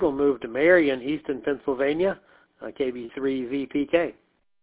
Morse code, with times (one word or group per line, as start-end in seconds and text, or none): none